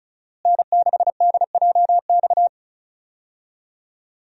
Morse code, heard as a keyed tone, about 36 wpm, 700 Hz. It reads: N6DJX
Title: USA Call Signs - 5 Characters - Repeated 2x - 35wpm